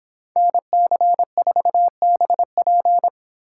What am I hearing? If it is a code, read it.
Morse code, heard as a keyed tone, about 30 wpm, 700 Hz. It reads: NC4BP